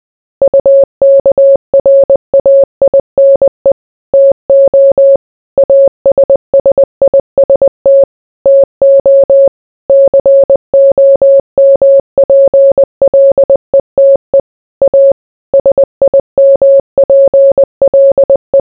Morse code, heard as a keyed tone, about 20 wpm, 550 Hz. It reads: UKRAINE TO ASSIST TO COMPLETE A SIMPLE